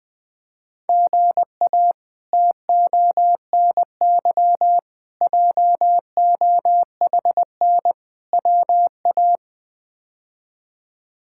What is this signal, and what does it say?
Morse code, heard as a keyed tone, about 20 wpm, 700 Hz. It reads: GA TONY JOHN WA
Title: K1USN SST QSO Practice - 20wpm